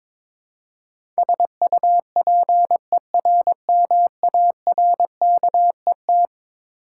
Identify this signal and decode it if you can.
Morse code, heard as a keyed tone, about 22 wpm, 700 Hz. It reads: SUPERMARKET
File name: practice / 22wpm / Eleven Letter Words - 22wpm